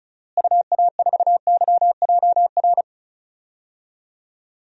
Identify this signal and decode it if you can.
Morse code, heard as a keyed tone, about 36 wpm, 700 Hz. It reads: UA4YJR
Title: DXCC-100 Call Signs - 6 Characters - Repeated 3x - 35wpm